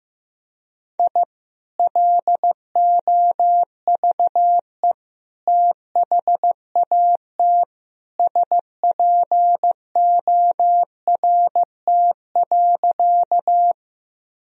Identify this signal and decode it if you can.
Morse code, heard as a keyed tone, about 16 wpm, 700 Hz. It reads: I LOVE THAT SPORT.
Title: Sentences from Top 1000 Words - Review Part 4 - 15wpm